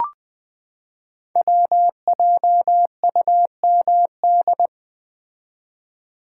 Morse code, heard as a keyed tone, about 20 wpm, 700 Hz. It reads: WJUMD
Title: Code Groups - 5 Letters - 20wpm